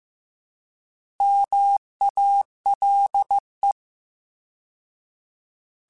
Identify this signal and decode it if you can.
Morse code, heard as a keyed tone, about 15 wpm, 800 Hz. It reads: MALE